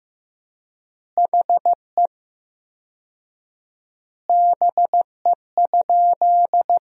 Morse code, heard as a keyed tone, about 15 wpm, 700 Hz. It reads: HE BE?
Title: 4x Word-Spacing - Sentences from Top 200 Words - 15wpm